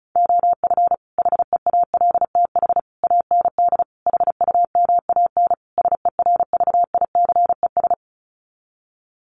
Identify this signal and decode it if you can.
Morse code, heard as a keyed tone, about 34 wpm, 700 Hz. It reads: OF HEALTH AND HUMAN SERVICES
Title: Sets of 5 Words 35wpm